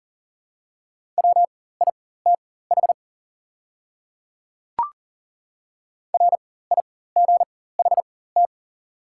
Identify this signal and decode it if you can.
Morse code, heard as a keyed tone, about 40 wpm, spaced as for 19 wpm, 690 Hz. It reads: WITH RIGHT
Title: 4x Character Spacing - Top 300 Words - 40wpm